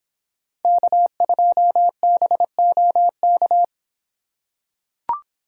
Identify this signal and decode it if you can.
Morse code, heard as a keyed tone, about 26 wpm, 700 Hz. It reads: K2BOK